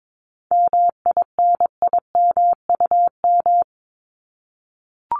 Morse code, heard as a keyed tone, about 22 wpm, 700 Hz. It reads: MINIMUM
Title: Top 1251-1500 Words 22wpm